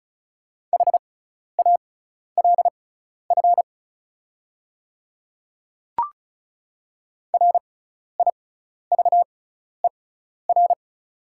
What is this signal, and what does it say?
Morse code, heard as a keyed tone, about 36 wpm, spaced as for 12 wpm, 690 Hz. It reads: HALF RIVER
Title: 6x Character Spacing - Top 500 Words - 35wpm